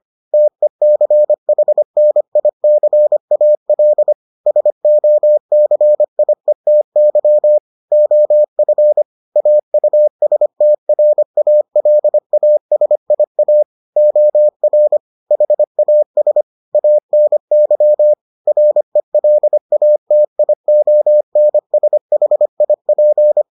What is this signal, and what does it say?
Morse code, heard as a keyed tone, about 25 wpm, 600 Hz. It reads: TECHNICAL SOCIETY OF AUSTRALASIA OR HAS ANY RELATIONSHIP